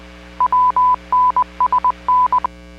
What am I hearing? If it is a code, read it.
Morse code, heard as a keyed tone, about 20 wpm, 1000 Hz. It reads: WNSN